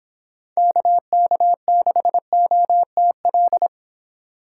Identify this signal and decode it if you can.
Morse code, heard as a keyed tone, about 29 wpm, 700 Hz. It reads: KK6OTL